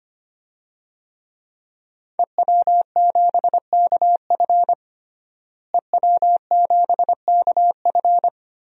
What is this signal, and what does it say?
Morse code, heard as a keyed tone, about 25 wpm, 700 Hz. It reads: EW7KF EW7KF
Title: DXCC-100 Call Signs - 5 Characters - Repeated 2x - 25wpm